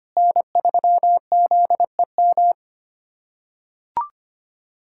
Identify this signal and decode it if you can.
Morse code, heard as a keyed tone, about 25 wpm, 700 Hz. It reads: N3ZEM
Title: USA Callsign DE Callsign - 5 Characters - 25wpm